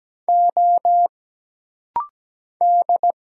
Morse code, heard as a keyed tone, about 17 wpm, 700 Hz. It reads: O D
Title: Single Letter-Number Alt - Rapid-Fire 17wpm